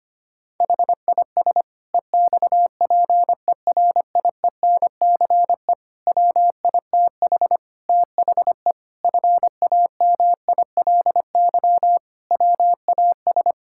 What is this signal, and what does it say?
Morse code, heard as a keyed tone, about 25 wpm, 700 Hz. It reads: HIS EXPERIENCE WITH THE FAMILY WAS
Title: Sentences from Top 1000 Words - Review Part 6 - 25wpm